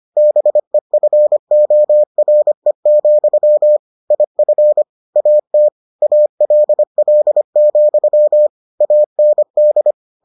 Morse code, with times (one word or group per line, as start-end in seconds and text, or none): 0.2-3.8 BEFORE,
4.1-4.8 IF
5.2-5.7 AT
6.0-8.5 ALL,
8.8-9.9 AND